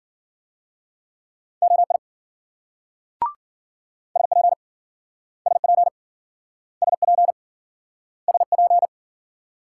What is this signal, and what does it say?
Morse code, heard as a keyed tone, about 53 wpm, 700 Hz. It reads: OI SP SP SP SP